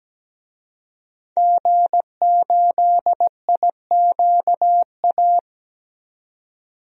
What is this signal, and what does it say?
Morse code, heard as a keyed tone, about 17 wpm, 700 Hz. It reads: G8IQA
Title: DXCC-100 Call Signs - 5 Characters - 17wpm